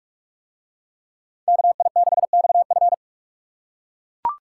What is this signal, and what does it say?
Morse code, heard as a keyed tone, about 44 wpm, 700 Hz. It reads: KI6XF